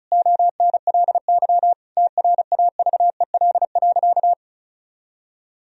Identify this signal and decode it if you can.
Morse code, heard as a keyed tone, about 34 wpm, 700 Hz. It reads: ONLY TRAVEL.
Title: Sentences from Top 1000 Words - Review Part 2 - 35wpm